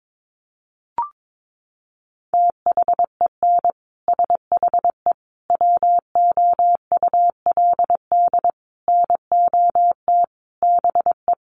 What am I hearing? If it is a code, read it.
Morse code, heard as a keyed tone, about 22 wpm, 700 Hz. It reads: THEN SHE WOULD NOT BE
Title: Sentences from Top 1400 Words 22wpm